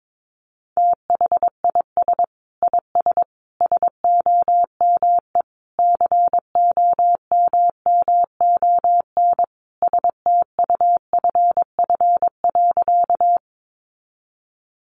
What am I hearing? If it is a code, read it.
Morse code, heard as a keyed tone, about 22 wpm, 700 Hz. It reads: THIS IS SOME COMMON STUFF.
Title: Sentences from Top 1400 Words 22wpm